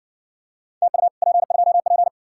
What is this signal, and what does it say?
Morse code, heard as a keyed tone, about 60 wpm, 700 Hz. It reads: TU P2P